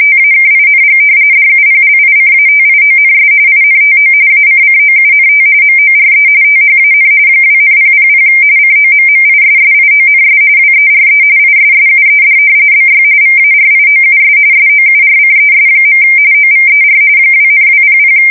RTTY, about 45.45 baud. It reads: TAKING DUPLICATIONS INTO ACCOUNT MORE THAN 2,000 COMMISSIONS WERE ISSUED BY VARIOUS SOURCES. OVER 2,200
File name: fsk170 / usnavy1